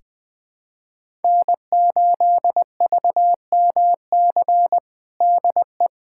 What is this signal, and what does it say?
Morse code, heard as a keyed tone, about 20 wpm, 700 Hz. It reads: N8VMC DE
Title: USA Callsign DE Callsign - 5 Characters - 20wpm